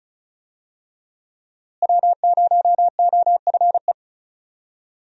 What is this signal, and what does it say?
Morse code, heard as a keyed tone, about 34 wpm, 700 Hz. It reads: W0OFE
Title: USA Call Signs - 5 Characters - Repeated 3x - 35wpm